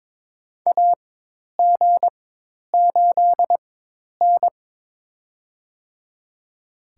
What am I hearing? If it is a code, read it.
Morse code, heard as a keyed tone, about 22 wpm, spaced as for 10 wpm, 700 Hz. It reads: AG8N
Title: USA Call Signs - 4 Characters - 4x Character Spacing - 22wpm